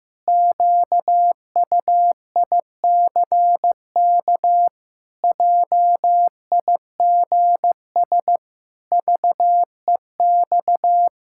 Morse code, heard as a keyed tone, about 15 wpm, 700 Hz. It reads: QUICK JIGS VEX
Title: Pangram Challenge v2 - 15wpm